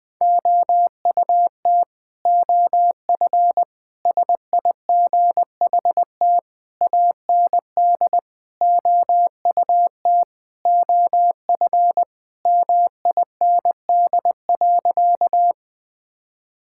Morse code, heard as a keyed tone, about 20 wpm, 700 Hz. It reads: OUT OF SIGHT AND OUT OF MIND.
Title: Sentences with 7-Word Phrases - Spoken Beginning - 20wpm